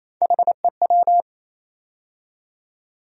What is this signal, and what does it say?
Morse code, heard as a keyed tone, about 28 wpm, 700 Hz. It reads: HEW
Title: USA Call Sign Postfixes - 3 Letters 28wpm